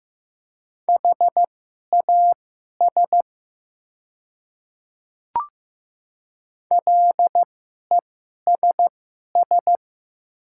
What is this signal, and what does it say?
Morse code, heard as a keyed tone, about 15 wpm, spaced as for 11 wpm, 700 Hz. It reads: HAS LESS
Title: 2x Character Spacing - Top 500 Words - 15wpm